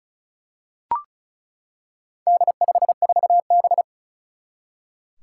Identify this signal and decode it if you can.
Morse code, heard as a keyed tone, about 36 wpm, 700 Hz. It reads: D54B
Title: Four Challenging Letter-Number Combinations 35wpm